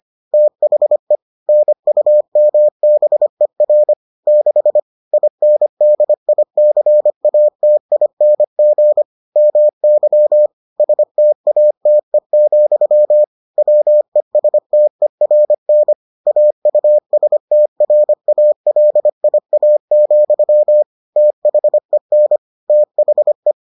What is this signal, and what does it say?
Morse code, heard as a keyed tone, about 25 wpm, 600 Hz. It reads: THE NUMBER 6 INDICATING MY STATE, WESTERN AUSTRALIA, THEN THE